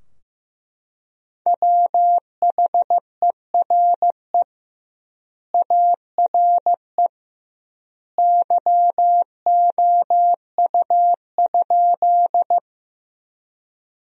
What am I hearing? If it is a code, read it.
Morse code, heard as a keyed tone, about 15 wpm, 700 Hz. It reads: WHERE ARE YOU?
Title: 2x Word-Spacing - Sentences from Top 200 Words - 15wpm